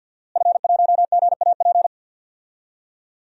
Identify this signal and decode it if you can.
Morse code, heard as a keyed tone, about 53 wpm, 700 Hz. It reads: F1GAP
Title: DXCC-100 Call Signs - 5 Characters - Repeated 3x - 50wpm